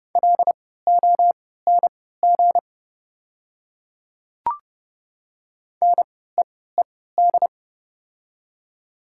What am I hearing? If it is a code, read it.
Morse code, heard as a keyed tone, about 30 wpm, spaced as for 17 wpm, 700 Hz. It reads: LONG NEED
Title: 3x Character Spacing - Top 300 Words - 30wpm